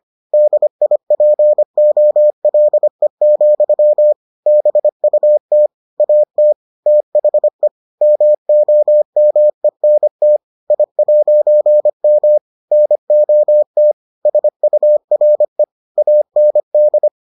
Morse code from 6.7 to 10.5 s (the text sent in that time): THE MOMENT